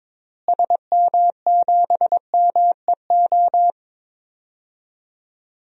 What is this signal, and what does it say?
Morse code, heard as a keyed tone, about 22 wpm, 700 Hz. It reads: SM7MEO
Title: DXCC-100 Call Signs - 6 Characters - Repeated 3x - 22wpm